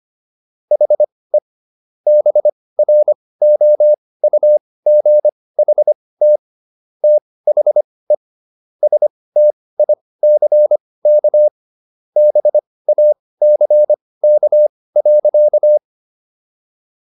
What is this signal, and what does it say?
Morse code, heard as a keyed tone, about 25 wpm, spaced as for 18 wpm, 600 Hz. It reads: HE BROUGHT THE STICK BACK.